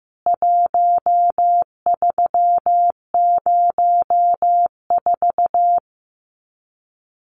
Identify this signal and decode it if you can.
Morse code, heard as a keyed tone, about 15 wpm, 700 Hz. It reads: 1304
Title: CWT Exchange 15wpm